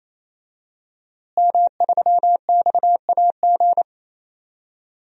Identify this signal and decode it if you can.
Morse code, heard as a keyed tone, about 28 wpm, 700 Hz. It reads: M3XAG
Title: DXCC-100 Call Signs - 5 Characters - Repeated 3x - 28wpm